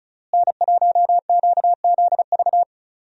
Morse code, heard as a keyed tone, about 36 wpm, 700 Hz. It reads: N1QZV
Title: USA Call Signs - 5 Characters - Repeated 2x - 35wpm